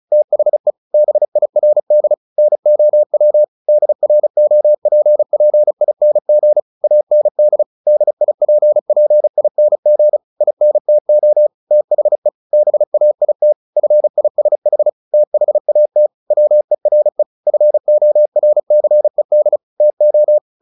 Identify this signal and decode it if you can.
Morse code, heard as a keyed tone, about 34 wpm, 600 Hz. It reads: THE BIRD NOW DROPPING AND DIPPING INTO THE BAIT FISH THAT WERE FORCED TO
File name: the old man and the sea / the old man and the sea 1 35wpm